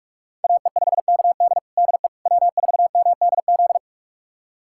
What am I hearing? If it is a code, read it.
Morse code, heard as a keyed tone, about 41 wpm, 700 Hz. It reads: AE5KD DE W4MDZ
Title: USA Callsign DE Callsign - 5 Characters - 45wpm